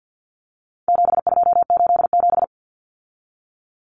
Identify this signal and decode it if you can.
Morse code, heard as a keyed tone, about 50 wpm, 700 Hz. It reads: 7287
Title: Common Cut Numbers - 4-Digit 50wpm